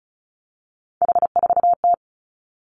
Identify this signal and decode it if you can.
Morse code, heard as a keyed tone, about 36 wpm, 700 Hz. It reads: H4T
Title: Three Challenging Letter-Number Combinations 35wpm